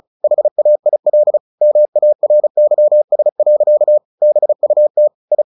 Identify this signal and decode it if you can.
Morse code, heard as a keyed tone, about 34 wpm, 600 Hz. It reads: HAIL MARYS. BUT I
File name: the old man and the sea 2 35wpm